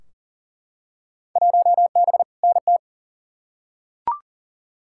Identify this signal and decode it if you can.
Morse code, heard as a keyed tone, about 40 wpm, 700 Hz. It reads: 1B NT
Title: Field Day Exchanges Encore - 40wpm